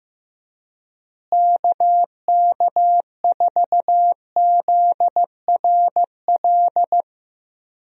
Morse code, heard as a keyed tone, about 15 wpm, 700 Hz. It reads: KK4ZRL
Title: USA Call Signs - 6 Characters - Repeated 2x - 15wpm